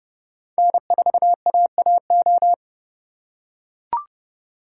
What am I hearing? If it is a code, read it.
Morse code, heard as a keyed tone, about 30 wpm, 690 Hz. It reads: N4AAO
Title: USA Call Signs - 5 Characters - 2x Character Spacing - 30wpm